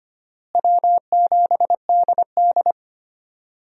Morse code, heard as a keyed tone, about 25 wpm, 700 Hz. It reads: W7DD
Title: USA Call Signs - 4 Characters - Repeated 3x - 25wpm